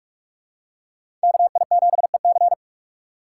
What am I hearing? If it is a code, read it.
Morse code, heard as a keyed tone, about 46 wpm, 700 Hz. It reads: KI7EC
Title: USA Call Signs - 3x Character Spacing - 45wpm